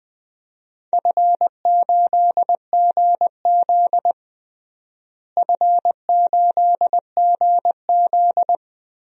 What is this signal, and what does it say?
Morse code, heard as a keyed tone, about 20 wpm, 700 Hz. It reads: F8GZ F8GZ
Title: DXCC-100 Call Signs - 4 Characters - Repeated 2x - 20wpm